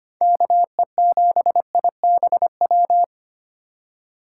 Morse code, heard as a keyed tone, about 32 wpm, 700 Hz. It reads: KE7IBW